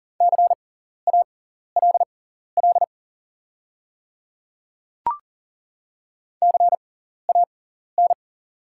Morse code, heard as a keyed tone, about 37 wpm, spaced as for 14 wpm, 690 Hz. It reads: CALL CAN
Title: 6x Character Spacing - Top 100 Words - 40wpm